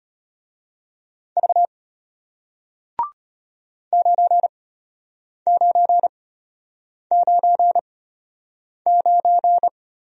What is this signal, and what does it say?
Morse code, heard as a keyed tone, about 34 wpm, spaced as for 8 wpm, 700 Hz. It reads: V 9999